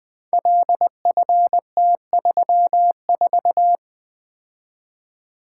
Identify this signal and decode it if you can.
Morse code, heard as a keyed tone, about 20 wpm, 700 Hz. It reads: LFT34
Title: Code Groups - 5 Letter:Numbers - 20wpm